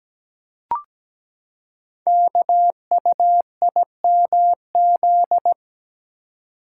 Morse code, heard as a keyed tone, about 17 wpm, 700 Hz. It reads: KUIMZ
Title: Code Groups - 5 Letters - 17wpm